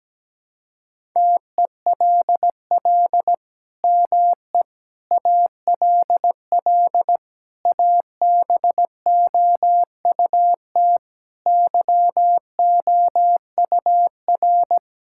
Morse code, heard as a keyed tone, about 17 wpm, 700 Hz. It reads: TELL ME ALL ABOUT YOUR
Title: Sentences from Top 1000 Words - Review Part 6 - 17wpm